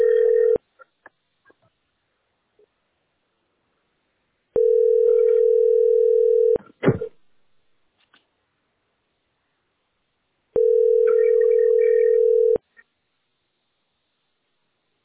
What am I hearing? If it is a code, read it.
Ringback tone (North America), 440 + 480 Hz.